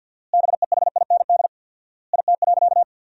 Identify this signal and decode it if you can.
Morse code, heard as a keyed tone, about 50 wpm, 700 Hz. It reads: BEHIND IT.
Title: 4x Word-Spacing - Sentences from Top 500 Words - 50wpm